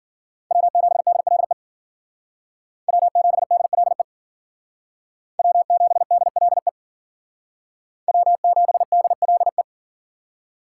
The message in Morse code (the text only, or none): W7DLE W7DLE W7DLE W7DLE